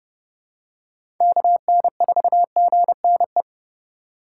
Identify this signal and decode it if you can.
Morse code, heard as a keyed tone, about 30 wpm, 700 Hz. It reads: KN4GNE
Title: USA Call Signs - 6 Characters - 6x Character Spacing - 30wpm